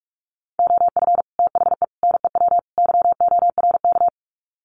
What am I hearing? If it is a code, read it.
Morse code, heard as a keyed tone, about 44 wpm, 700 Hz. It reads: OF THE NEW YORK
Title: Sets of 4 Words 45wpm